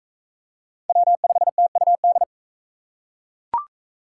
Morse code, heard as a keyed tone, about 38 wpm, 700 Hz. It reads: W5TUD